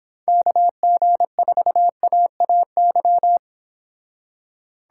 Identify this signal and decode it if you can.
Morse code, heard as a keyed tone, about 30 wpm, 700 Hz. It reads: KG4AAY